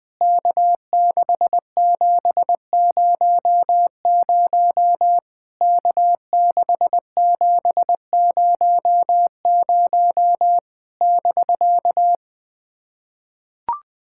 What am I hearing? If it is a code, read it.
Morse code, heard as a keyed tone, about 21 wpm, 700 Hz. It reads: K6700 K6700 <BK>
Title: POTA QSO Practice - 20wpm